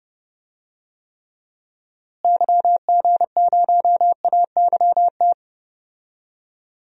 Morse code, heard as a keyed tone, about 32 wpm, 700 Hz. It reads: YG0AYT
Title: DXCC-100 Call Signs - 6 Characters - Repeated 3x - 30wpm